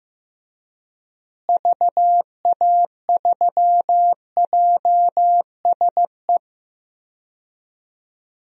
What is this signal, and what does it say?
Morse code, heard as a keyed tone, about 15 wpm, 700 Hz. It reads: VA3JSE
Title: DXCC-100 Call Signs - 6 Characters - Repeated 3x - 15wpm